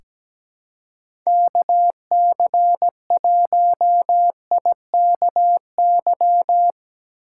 Morse code, heard as a keyed tone, about 17 wpm, 700 Hz. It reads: KC1IKY
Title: USA Call Signs - 2x Character Spacing - 17wpm